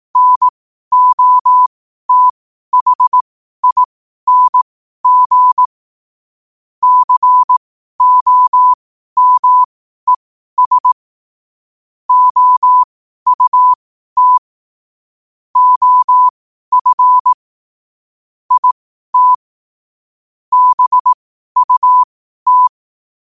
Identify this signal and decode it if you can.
Morse code, heard as a keyed tone, about 18 wpm, spaced as for 12 wpm, 1000 Hz. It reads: NOTHING COMES OUT OF IT BUT
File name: qotdfiles / 12wpm